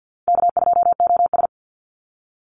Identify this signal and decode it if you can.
Morse code, heard as a keyed tone, about 49 wpm, 700 Hz. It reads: K2OS
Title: Call Signs Repeated 3x - 50wpm